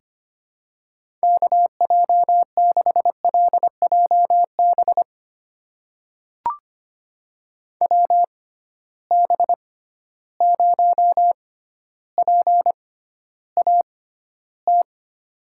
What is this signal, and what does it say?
Morse code, heard as a keyed tone, about 25 wpm, spaced as for 13 wpm, 700 Hz. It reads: KJ6LJB WB0PAT
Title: USA Call Signs - 6 Characters - 6x Character Spacing - 25wpm